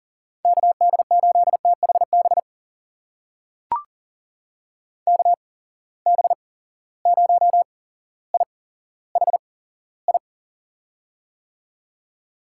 Morse code, heard as a keyed tone, about 40 wpm, spaced as for 17 wpm, 700 Hz. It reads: KD8THB KB0IHI
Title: USA Call Signs - 6 Characters - 8x Character Spacing - 40wpm